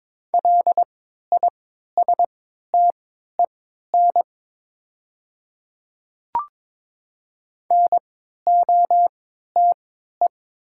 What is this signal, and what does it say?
Morse code, heard as a keyed tone, about 22 wpm, spaced as for 13 wpm, 700 Hz. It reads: LISTEN NOTE
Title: 3x Character Spacing - Top 500 Words - 22wpm